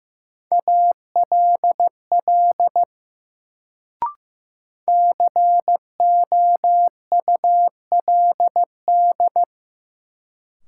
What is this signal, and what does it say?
Morse code, heard as a keyed tone, about 15 wpm, 700 Hz. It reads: ALL COULD
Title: Sentences from Top 100 Words - New Word Drill - 15wpm